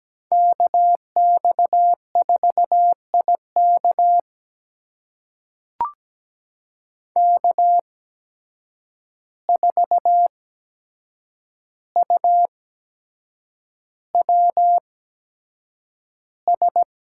Morse code, heard as a keyed tone, about 17 wpm, spaced as for 7 wpm, 700 Hz. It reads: KX4IK K4UWS
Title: USA Call Signs - 5 Characters - 8x Character Spacing - 17wpm